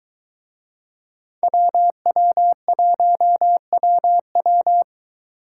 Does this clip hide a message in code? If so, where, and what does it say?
Yes: Morse code from 1.4 to 4.8 s, heard as a keyed tone, about 21 wpm, 700 Hz. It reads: WW1WW